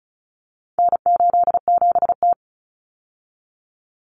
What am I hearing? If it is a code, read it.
Morse code, heard as a keyed tone, about 34 wpm, 700 Hz. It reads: N87T
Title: Common Cut Numbers - 4-Digit 35wpm